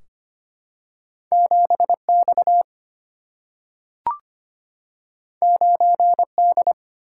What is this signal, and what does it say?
Morse code, heard as a keyed tone, about 25 wpm, 700 Hz. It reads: 7 X 9 D
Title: Single-Letter Single-Number 25wpm